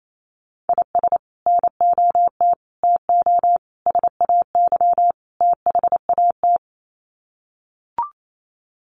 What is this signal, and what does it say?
Morse code, heard as a keyed tone, about 28 wpm, 700 Hz. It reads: IS NOT TO SAY THAT
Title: Sets of 5 Words 28wpm